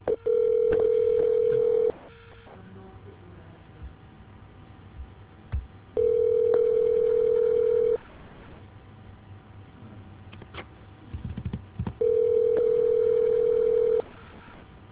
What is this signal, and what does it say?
Ringback tone (North America), 440 + 480 Hz.